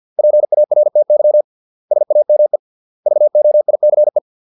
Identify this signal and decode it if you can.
Morse code, heard as a keyed tone, about 50 wpm, 600 Hz. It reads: PARTY SAME VOICE